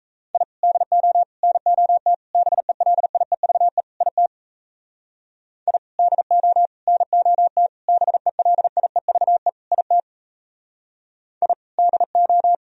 Morse code, heard as a keyed tone, about 35 wpm, 700 Hz. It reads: I DO NOT BELIEVE IT I DO NOT BELIEVE IT I DO